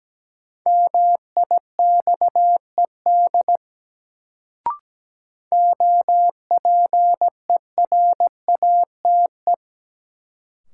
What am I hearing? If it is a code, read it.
Morse code, heard as a keyed tone, about 17 wpm, 700 Hz. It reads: MIXED OPERATE
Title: Sentences from Top 1500 Words - New Word Drill - 17wpm